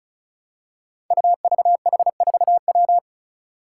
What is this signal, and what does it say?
Morse code, heard as a keyed tone, about 34 wpm, 700 Hz. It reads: UVH4W
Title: Five Challenging Letter-Number Combinations 35wpm